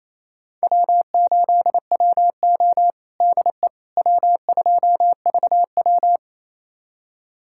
Morse code, heard as a keyed tone, about 28 wpm, 700 Hz. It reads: W8WO DE W2VW
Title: USA Callsign DE Callsign - 4 Characters - 28wpm